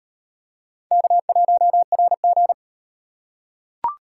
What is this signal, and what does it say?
Morse code, heard as a keyed tone, about 38 wpm, 700 Hz. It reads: K1RG